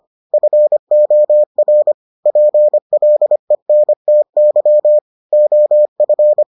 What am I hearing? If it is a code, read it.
Morse code, heard as a keyed tone, about 25 wpm, 600 Hz. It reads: FOR PLENTY OF